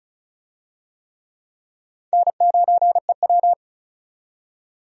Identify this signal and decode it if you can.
Morse code, heard as a keyed tone, about 36 wpm, 700 Hz. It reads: N9EW
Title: USA Call Signs - 4 Characters - 35wpm